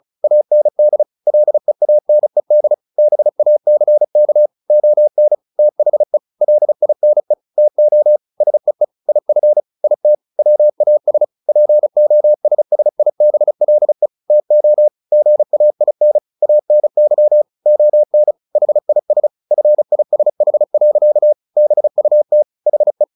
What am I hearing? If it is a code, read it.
Morse code, heard as a keyed tone, about 34 wpm, 600 Hz. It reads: AND LEANED BACK ON THE LINE TO SEE IF IT WAS POSSIBLE TO GAIN ANY ON HIS FISH. BUT HE
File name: the old man and the sea 2 35wpm